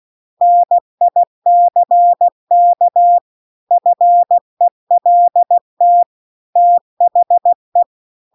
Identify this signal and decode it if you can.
Morse code, heard as a keyed tone, about 16 wpm, 700 Hz. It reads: NICK FELT THE